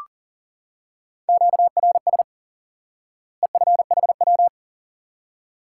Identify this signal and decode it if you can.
Morse code, heard as a keyed tone, about 40 wpm, 700 Hz. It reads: QRS EFHW
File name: course / instant-qso-element-course-3 / IQE Course 3 - Lesson 89 - YAGI - Two Elements - 40wpm